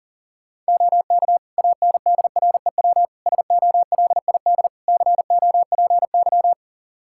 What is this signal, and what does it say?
Morse code, heard as a keyed tone, about 37 wpm, 700 Hz. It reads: OK ANDREW SOLID COPY
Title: Practice QSO - Exchange 3 - More Info - 40wpm